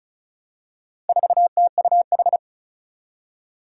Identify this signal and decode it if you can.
Morse code, heard as a keyed tone, about 36 wpm, 700 Hz. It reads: 4TUH
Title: Four Challenging Letter-Number Combinations 35wpm